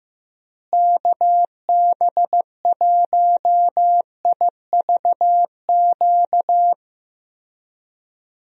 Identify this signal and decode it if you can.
Morse code, heard as a keyed tone, about 15 wpm, 700 Hz. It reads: KB1IVQ
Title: USA Call Signs - 6 Characters - Repeated 3x - 15wpm